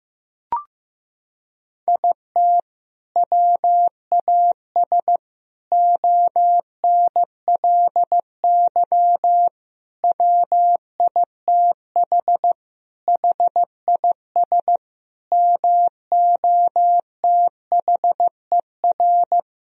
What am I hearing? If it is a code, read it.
Morse code, heard as a keyed tone, about 15 wpm, 700 Hz. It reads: IT WAS ONLY WITH HIS MOTHER
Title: Sentences from Top 1000 Words - Review Part 1 - 15wpm